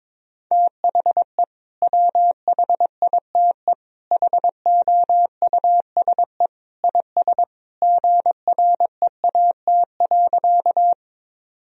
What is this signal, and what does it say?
Morse code, heard as a keyed tone, about 22 wpm, 700 Hz. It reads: THE WHITE HOUSE IS GREAT.
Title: Sentences from Top 1000 Words - Review Part 1 - 22wpm